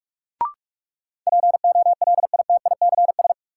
Morse code, heard as a keyed tone, about 43 wpm, 700 Hz. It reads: POLITICS
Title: Sentences from Top 1000 Words - New Word Drill - 45wpm